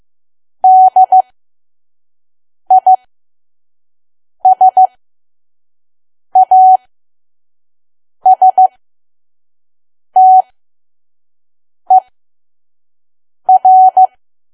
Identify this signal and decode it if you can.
Morse code, heard as a keyed tone, about 15 wpm, spaced as for 5 wpm, 750 Hz. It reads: DISASTER
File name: Archive / 5 WPM files / PM